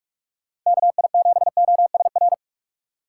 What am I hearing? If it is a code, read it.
Morse code, heard as a keyed tone, about 46 wpm, 700 Hz. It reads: KI7OSR
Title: USA Call Signs - 6 Characters - 3x Character Spacing - 45wpm